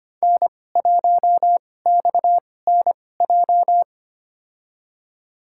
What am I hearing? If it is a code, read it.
Morse code, heard as a keyed tone, about 25 wpm, spaced as for 18 wpm, 700 Hz. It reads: N1XNJ
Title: USA Call Signs - 5 Characters - 2x Character Spacing - 25wpm